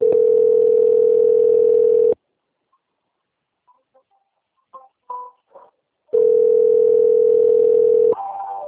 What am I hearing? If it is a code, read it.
Ringback tone (North America), 440 + 480 Hz.